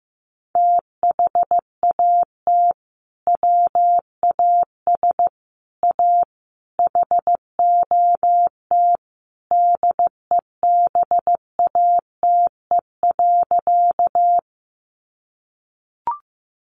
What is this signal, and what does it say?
Morse code, heard as a keyed tone, about 14 wpm, 700 Hz. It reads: THAT WAS A HOT DEBATE.
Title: Sentences from Top 1100 Words 15wpm